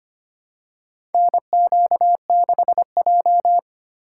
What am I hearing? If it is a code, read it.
Morse code, heard as a keyed tone, about 25 wpm, 700 Hz. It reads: NQ6J
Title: USA Call Signs - 4 Characters - 8x Character Spacing - 25wpm